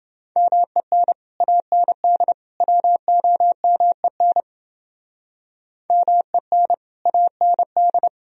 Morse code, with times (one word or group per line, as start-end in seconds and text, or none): 0.4-1.1 MEN
1.4-2.3 AND
2.6-4.4 WOMEN
5.9-6.7 MEN
7.1-8.1 AND